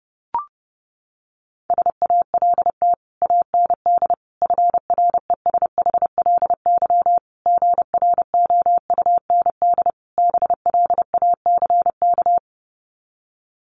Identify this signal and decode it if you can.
Morse code, heard as a keyed tone, about 30 wpm, 700 Hz. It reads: SALT AND FRESHLY GROUND BLACK
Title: Sets of 5 Words Encore 30wpm